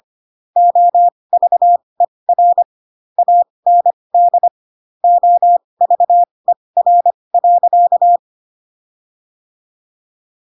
Morse code, heard as a keyed tone, about 25 wpm, spaced as for 20 wpm, 700 Hz. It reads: OVER AND OVER.
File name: Pinocchio / Pinocchio ch8 15 25 ik1hge